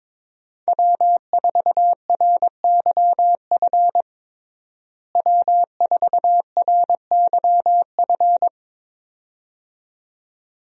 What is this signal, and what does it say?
Morse code, heard as a keyed tone, about 22 wpm, 700 Hz. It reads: W4RYF W4RYF
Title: USA Call Signs - 5 Characters - Repeated 2x - 22wpm